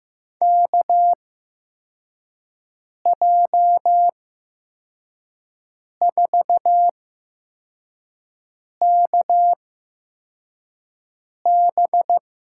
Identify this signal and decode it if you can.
Morse code, heard as a keyed tone, about 15 wpm, spaced as for 4 wpm, 700 Hz. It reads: KJ4KB
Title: USA Call Signs - 8x Character Spacing - 15wpm